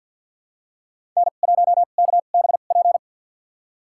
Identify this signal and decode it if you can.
Morse code, heard as a keyed tone, about 53 wpm, spaced as for 37 wpm, 700 Hz. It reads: N1KBP
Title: USA Call Signs - 5 Characters - 2x Character Spacing - 50wpm